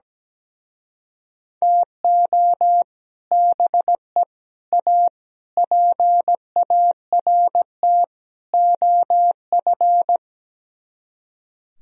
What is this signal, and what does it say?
Morse code, heard as a keyed tone, about 17 wpm, 700 Hz. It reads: TO BE A PART OF
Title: Speed-Racing - Sets of 5 Words 17wpm